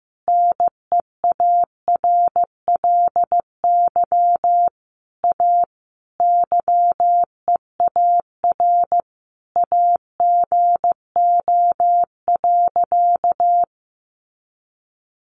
Morse code, heard as a keyed tone, about 15 wpm, 700 Hz. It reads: NEARLY A YEAR AGO.
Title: Sentences from Top 1300 Words 15wpm